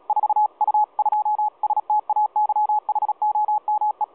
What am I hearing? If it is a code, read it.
Morse code, heard as a keyed tone, about 36 wpm, 850 Hz. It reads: 4U2STAYHOME
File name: 4U2STAYHOME 7016.2kHz